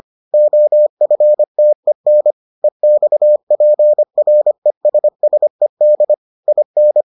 Morse code, heard as a keyed tone, about 25 wpm, 600 Hz. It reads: OFTEN EXPRESSED IN